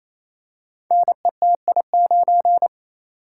Morse code, heard as a keyed tone, about 27 wpm, 700 Hz. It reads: NETI9
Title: Code Groups - 5 Letter:Numbers - 28wpm